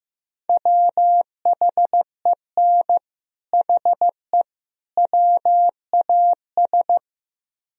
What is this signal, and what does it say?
Morse code, heard as a keyed tone, about 15 wpm, 700 Hz. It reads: WHEN HE WAS
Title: Speed-Racing - Sets of 3 Words 15wpm